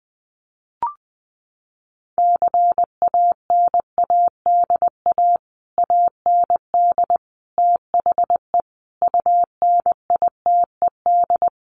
Morse code, heard as a keyed tone, about 20 wpm, 700 Hz. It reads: CANADA AND THE UNITED
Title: Sets of 5 Words Encore 20wpm